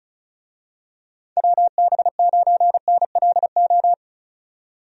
Morse code, heard as a keyed tone, about 36 wpm, 700 Hz. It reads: WB9NLO
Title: USA Call Signs - 6 Characters - 35wpm